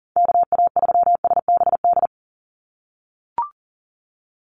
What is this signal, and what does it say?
Morse code, heard as a keyed tone, about 40 wpm, 700 Hz. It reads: KA3SBD
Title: USA Call Signs 40wpm